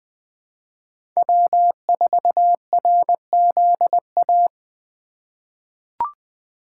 Morse code, heard as a keyed tone, about 20 wpm, 700 Hz. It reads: W4RZA
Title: USA Call Signs - 5 Characters - 3x Character Spacing - 20wpm